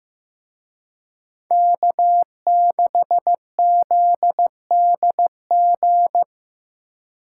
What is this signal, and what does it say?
Morse code, heard as a keyed tone, about 16 wpm, 700 Hz. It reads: K6ZDG
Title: Speed Racing - USA 1x3 Call Signs 15wpm